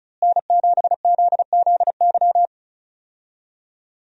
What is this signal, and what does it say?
Morse code, heard as a keyed tone, about 35 wpm, 700 Hz. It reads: N7ZZY
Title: USA Call Signs - 5 Characters - Repeated 3x - 35wpm